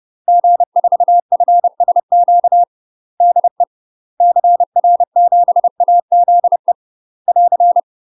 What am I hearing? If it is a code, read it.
Morse code, heard as a keyed tone, about 30 wpm, 700 Hz. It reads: G4FSQ DE CR7AZE <AR>
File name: UK MASTER QSOs TRAINING / 30WPM / UK MASTER QSOs 0019